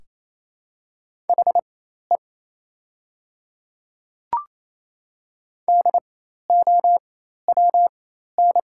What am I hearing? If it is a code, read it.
Morse code, heard as a keyed tone, about 28 wpm, spaced as for 13 wpm, 700 Hz. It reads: HE DOWN
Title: 4x Character Spacing - Top 100 Words - 28wpm